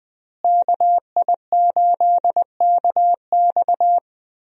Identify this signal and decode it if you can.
Morse code, heard as a keyed tone, about 20 wpm, 700 Hz. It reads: KI8KX
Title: USA Call Signs - 5 Characters - Repeated 3x - 20wpm